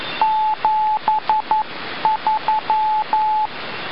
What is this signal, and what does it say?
Morse code, heard as a keyed tone, about 11 wpm, 860 Hz, with radio noise. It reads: 73
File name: luxorion / Radio / mode-beacon-va3grr